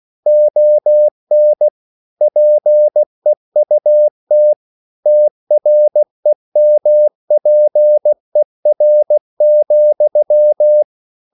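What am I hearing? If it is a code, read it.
Morse code, heard as a keyed tone, about 16 wpm, 600 Hz. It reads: ON PEUT TREMPER,
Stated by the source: Slow (QRS) Morse code at 16 wpm, from CW med Gnister.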